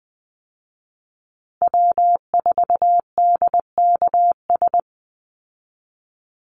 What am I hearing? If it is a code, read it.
Morse code, heard as a keyed tone, about 20 wpm, 700 Hz. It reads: W4DKS
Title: Call Signs Repeated 3x - 20wpm